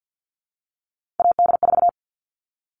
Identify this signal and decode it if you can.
Morse code, heard as a keyed tone, about 50 wpm, 700 Hz. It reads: AD4
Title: All Cut Numbers - 3-Digit 50wpm